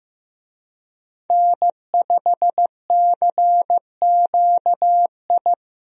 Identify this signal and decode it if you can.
Morse code, heard as a keyed tone, about 15 wpm, 700 Hz. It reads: N5CQI
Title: USA Call Signs - 5 Characters - Repeated 3x - 15wpm